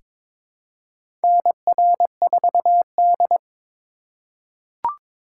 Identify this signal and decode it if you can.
Morse code, heard as a keyed tone, about 22 wpm, 700 Hz. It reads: NR4D
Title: USA Call Signs - 4 Characters - 3x Character Spacing - 22wpm